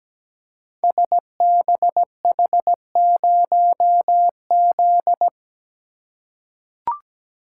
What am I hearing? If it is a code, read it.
Morse code, heard as a keyed tone, about 17 wpm, 690 Hz. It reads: SBH0Z
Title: Code Groups - 5 Letter:Numbers - 17wpm